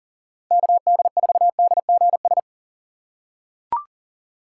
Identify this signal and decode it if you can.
Morse code, heard as a keyed tone, about 40 wpm, 700 Hz. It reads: KD4DGS
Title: USA Call Signs - 6 Characters - 4x Character Spacing - 40wpm